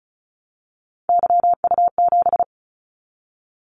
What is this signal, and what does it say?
Morse code, heard as a keyed tone, about 36 wpm, 700 Hz. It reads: YU7
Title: Callsign Prefixes for Top 25 DXCC - 35wpm